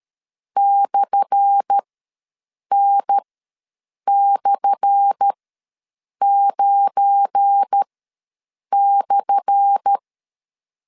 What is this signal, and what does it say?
Morse code, heard as a keyed tone, about 13 wpm, spaced as for 7 wpm, 800 Hz. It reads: /N/9/
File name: LEZIONI / 10 SIGNS / 10A SLASH ws 12wpm 3443